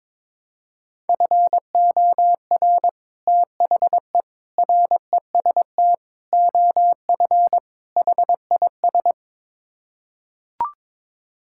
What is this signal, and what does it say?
Morse code, heard as a keyed tone, about 22 wpm, 700 Hz. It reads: FOR THE REST OF HIS
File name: practice / 22wpm / Build A Phrase - 5-Word Phrases - 22wpm